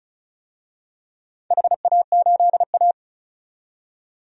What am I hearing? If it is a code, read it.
Morse code, heard as a keyed tone, about 36 wpm, 700 Hz. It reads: HA8A
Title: DXCC-100 Call Signs - 4 Characters - 35wpm